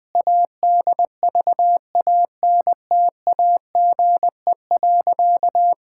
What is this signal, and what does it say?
Morse code, heard as a keyed tone, about 21 wpm, 700 Hz. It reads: ADVANTAGE.
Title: Sentences from Top 1000 Words - Review Part 6 - 20wpm